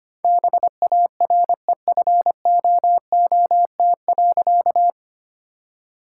Morse code, heard as a keyed tone, about 25 wpm, 700 Hz. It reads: BAREFOOT.
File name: practice / 25wpm / All-Age Appropriate Jokes - 25wpm